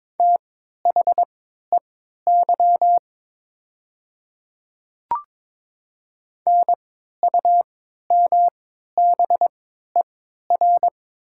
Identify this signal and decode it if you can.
Morse code, heard as a keyed tone, about 22 wpm, spaced as for 12 wpm, 700 Hz. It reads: THEY NUMBER
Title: 3x Character Spacing - Top 300 Words - 22wpm